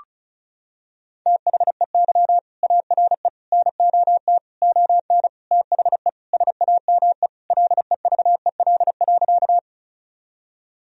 Morse code, heard as a keyed tone, about 36 wpm, 700 Hz. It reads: THEY ARE NOT ON THE SAME LEVEL.
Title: Sentences from Top 1000 Words - Review Part 4 - 35wpm